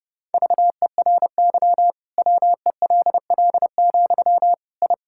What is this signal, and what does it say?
Morse code, heard as a keyed tone, about 30 wpm, 700 Hz. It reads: VERY WELL, I
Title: Sentences from Top 1000 Words - Review Part 5 - 30wpm